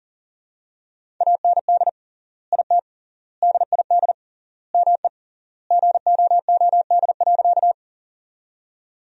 Morse code, heard as a keyed tone, about 40 wpm, 700 Hz. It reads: AND IT DID ME GOOD.
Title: 3x Word-Spacing - Sentences from Top 200 Words - 40wpm